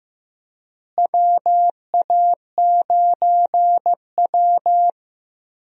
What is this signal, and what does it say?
Morse code, heard as a keyed tone, about 15 wpm, 700 Hz. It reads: WA9W
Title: USA Call Signs - 4 Characters - Repeated 3x - 15wpm